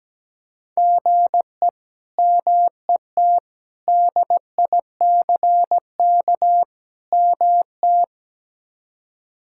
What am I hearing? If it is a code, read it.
Morse code, heard as a keyed tone, about 17 wpm, 700 Hz. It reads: GE MET DICK MT
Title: K1USN SST QSO Practice - 17wpm